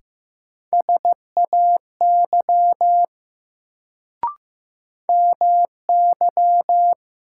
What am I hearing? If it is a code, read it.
Morse code, heard as a keyed tone, about 15 wpm, 700 Hz. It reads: SAY MY
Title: Sentences from Top 200 Words - New Word Drill - 15wpm